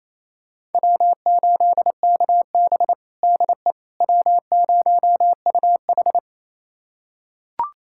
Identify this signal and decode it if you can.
Morse code, heard as a keyed tone, about 27 wpm, 700 Hz. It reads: W8KB DE W0UH
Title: USA Callsign DE Callsign - 4 Characters - 28wpm